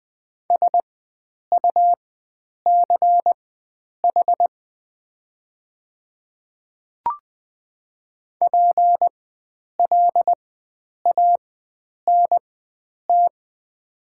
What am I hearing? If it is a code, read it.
Morse code, heard as a keyed tone, about 20 wpm, spaced as for 9 wpm, 700 Hz. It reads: SUCH PLANT
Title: 4x Character Spacing - Top 500 Words - 20wpm